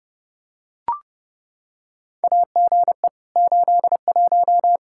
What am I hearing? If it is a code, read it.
Morse code, heard as a keyed tone, about 30 wpm, 700 Hz. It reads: AGE 81
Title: Practice QSO - Exchange 3 - More Info - 30wpm